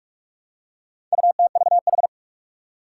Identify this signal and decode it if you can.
Morse code, heard as a keyed tone, about 44 wpm, 700 Hz. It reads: UTVH
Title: Four Challenging Letter-Number Combinations 45wpm